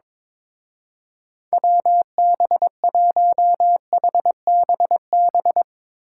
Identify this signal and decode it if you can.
Morse code, heard as a keyed tone, about 24 wpm, 700 Hz. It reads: WB1HBB
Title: Speed Racing - USA 2x3 Call Signs 22wpm